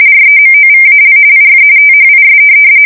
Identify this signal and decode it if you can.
RTTY, about 45.45 baud. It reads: REMOVAL IF THIS
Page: 170 Shift 2125/2295